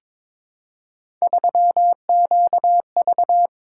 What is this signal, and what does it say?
Morse code, heard as a keyed tone, about 22 wpm, 700 Hz. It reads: 3QV
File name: practice / 22wpm / Three Letter-Number Combinations - 22wpm